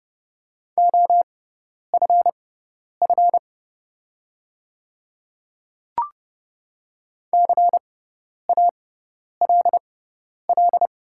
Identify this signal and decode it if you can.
Morse code, heard as a keyed tone, about 32 wpm, spaced as for 10 wpm, 700 Hz. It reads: OFF CALL
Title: 6x Character Spacing - Top 200 Words - 30wpm